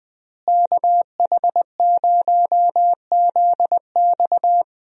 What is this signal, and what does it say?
Morse code, heard as a keyed tone, about 20 wpm, 700 Hz. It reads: KH0ZX
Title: USA Call Signs - 5 Characters - Repeated 3x - 20wpm